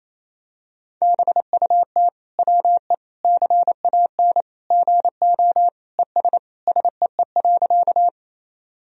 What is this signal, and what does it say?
Morse code, heard as a keyed tone, about 28 wpm, 700 Hz. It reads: BUT WE CAN GO ES SEE.
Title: Sentences from Top 100 Words and Abbreviations 28wpm